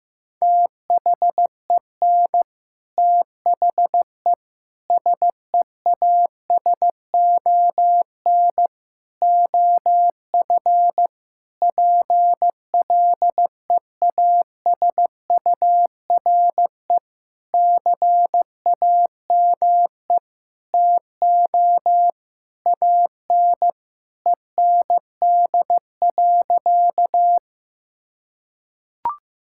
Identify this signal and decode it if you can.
Morse code, heard as a keyed tone, about 15 wpm, 700 Hz. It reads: THEN THE SEASON OF PLEASURE CAME TO AN END.
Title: Sentences from Top 1600 Words 15wpm